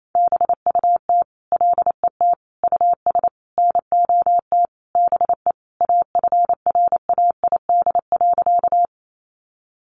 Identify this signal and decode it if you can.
Morse code, heard as a keyed tone, about 28 wpm, 700 Hz. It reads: BUT LET US NOT BE AFRAID.
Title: Sentences from Top 1500 Words 28wpm